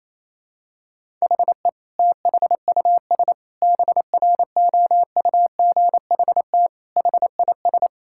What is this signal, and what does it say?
Morse code, heard as a keyed tone, about 27 wpm, 700 Hz. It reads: HE THUS BROUGHT HIS
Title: Sentences from Top 1000 Words - Review Part 5 - 28wpm